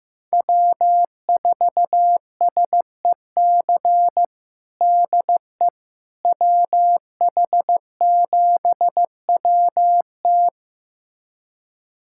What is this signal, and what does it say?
Morse code, heard as a keyed tone, about 15 wpm, 700 Hz. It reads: W4SEC DE WH7WT
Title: USA Callsign DE Callsign - 5 Characters - 15wpm